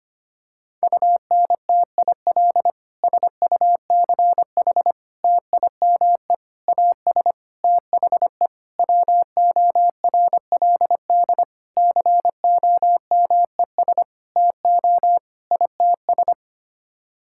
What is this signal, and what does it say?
Morse code, heard as a keyed tone, about 25 wpm, 700 Hz. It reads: UNTIL SUCH TIME AS THE WORLD COMES TO ITS
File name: Sets of 9 Words - 25wpm